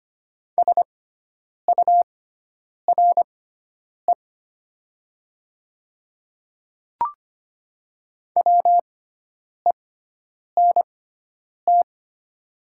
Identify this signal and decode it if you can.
Morse code, heard as a keyed tone, about 25 wpm, spaced as for 9 wpm, 700 Hz. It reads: SURE WENT
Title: 6x Character Spacing - Top 300 Words - 25wpm